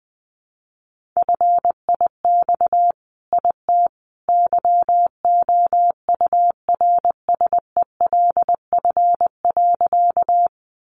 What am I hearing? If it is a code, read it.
Morse code, heard as a keyed tone, about 20 wpm, 700 Hz. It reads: FIX IT YOURSELF.
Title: Sentences from Top 1400 Words 20wpm